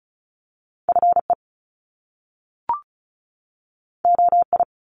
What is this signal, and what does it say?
Morse code, heard as a keyed tone, about 36 wpm, 700 Hz. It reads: FE OI
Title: Top Two Letter Combinations in Words 35wpm